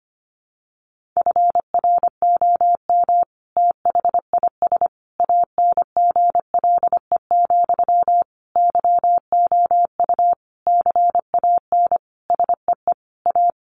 Morse code, heard as a keyed tone, about 25 wpm, 700 Hz. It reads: FROM THIS ANGLE, YOU CAN SEE A
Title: Sentences from Top 1500 Words 25wpm